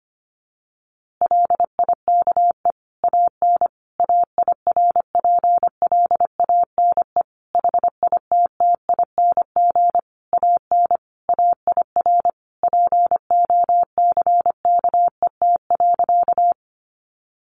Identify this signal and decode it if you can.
Morse code, heard as a keyed tone, about 25 wpm, 700 Hz. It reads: LIKE AN AIRPLANE HITTING AN AIR POCKET.
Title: A Sound Of Thunder 25wpm